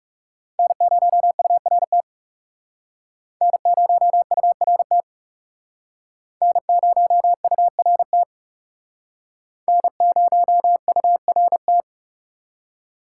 Morse code, heard as a keyed tone, about 40 wpm, 700 Hz. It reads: N0URT N0URT N0URT N0URT